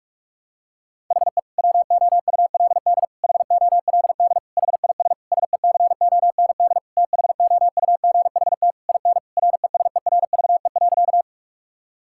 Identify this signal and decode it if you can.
Morse code, heard as a keyed tone, about 46 wpm, 700 Hz. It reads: HE WOULD HOLD HIS SECOND THOUGHT IN RESERVE.
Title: Sentences from Top 1000 Words - Review Part 6 - 45wpm